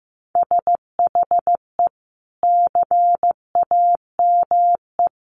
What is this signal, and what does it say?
Morse code, heard as a keyed tone, about 15 wpm, 700 Hz. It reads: SHE CAME
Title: Sentences from Top 300 Words 15wpm